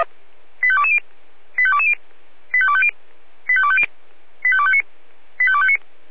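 Five-tone selective call (ZVEI), heard as quick runs of five tones, tones of about 70 ms. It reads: 85220 85220 85250 85250 85259 85259
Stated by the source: SDR-received ZVEI five-tone selective call.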